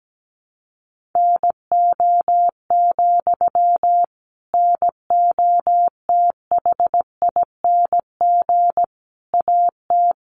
Morse code, heard as a keyed tone, about 17 wpm, 700 Hz. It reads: NO, NOTHING AT
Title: Sentences from Top 500 Words 17wpm